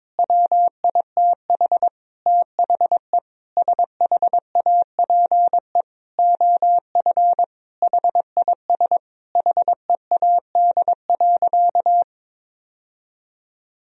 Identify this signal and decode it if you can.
Morse code, heard as a keyed tone, about 22 wpm, 700 Hz. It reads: WITH THE SHAPE OF HIS HEAD.
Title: Sentences from Top 1000 Words - Review Part 3 - 22wpm